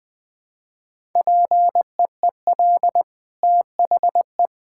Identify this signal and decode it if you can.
Morse code, heard as a keyed tone, about 21 wpm, 700 Hz. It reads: PEEL THE
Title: Easy Sentences with Hints - 20wpm